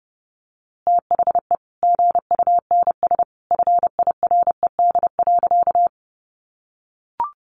Morse code, heard as a keyed tone, about 30 wpm, 700 Hz. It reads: THE GUNS FIRED.
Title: A Sound Of Thunder 30wpm